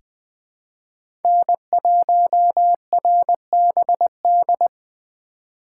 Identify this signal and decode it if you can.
Morse code, heard as a keyed tone, about 20 wpm, 700 Hz. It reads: N1RBD
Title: K1USN SST QSO Practice - 20wpm